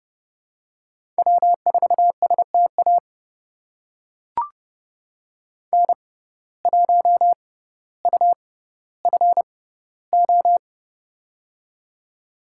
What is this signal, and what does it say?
Morse code, heard as a keyed tone, about 30 wpm, spaced as for 15 wpm, 700 Hz. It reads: W4STA N1UFO
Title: USA Call Signs - 5 Characters - 6x Character Spacing - 30wpm